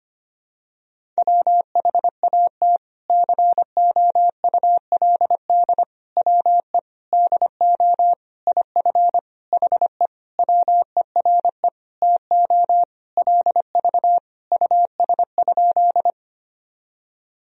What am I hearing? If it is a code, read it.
Morse code, heard as a keyed tone, about 25 wpm, 700 Hz. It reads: WHAT COULD WE DO IF HE WERE TO LV US?
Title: Sentences from Top 400 Words and Abbreviations 25wpm